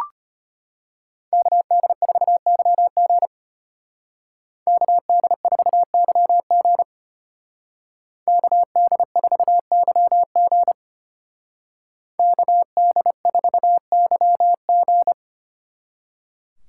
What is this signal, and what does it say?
Morse code, heard as a keyed tone, about 30 wpm, 700 Hz. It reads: KD4YG KD4YG KD4YG KD4YG
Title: Speed Racing - USA 2x2 Call Signs 25wpm